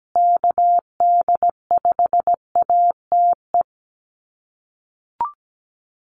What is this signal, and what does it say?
Morse code, heard as a keyed tone, about 17 wpm, 700 Hz. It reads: KD5ATE
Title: USA Call Signs Repeated 2x 17wpm